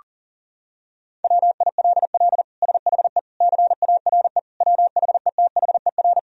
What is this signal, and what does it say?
Morse code, heard as a keyed tone, about 40 wpm, 700 Hz. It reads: WILL SHE CARE WHETHER
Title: Sentences from Top 1000 Words - Review Part 3 - 40wpm